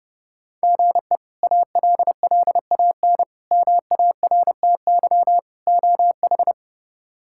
Morse code, heard as a keyed tone, about 30 wpm, 700 Hz. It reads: GE ALLAN MARTY OH
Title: K1USN SST QSO Practice - 30wpm